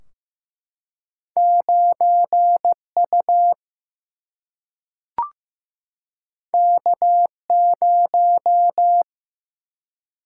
Morse code, heard as a keyed tone, about 15 wpm, 700 Hz. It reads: 9 U K 0
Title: Single-Letter Single-Number 15wpm